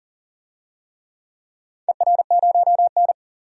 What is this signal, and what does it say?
Morse code, heard as a keyed tone, about 40 wpm, 700 Hz. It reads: ER0N
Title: DXCC-100 Call Signs - 4 Characters - 40wpm